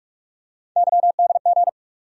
Morse code, heard as a keyed tone, about 47 wpm, 700 Hz. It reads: YDG
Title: USA Call Sign Postfixes - 3 Letters 45wpm